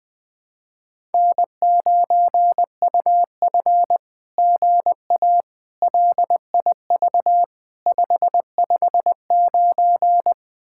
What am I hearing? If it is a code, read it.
Morse code, heard as a keyed tone, about 20 wpm, 700 Hz. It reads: N9UF GA LIV 559
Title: SOTA QSO Practice 20wpm